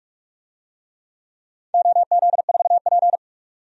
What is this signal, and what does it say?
Morse code, heard as a keyed tone, about 48 wpm, 700 Hz. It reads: OZ4P
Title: DXCC-100 Call Signs - 4 Characters - 45wpm